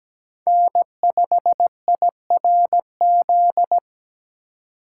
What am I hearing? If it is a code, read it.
Morse code, heard as a keyed tone, about 17 wpm, 700 Hz. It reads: N5IRZ
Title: USA Call Signs - 5 Characters - 2x Character Spacing - 17wpm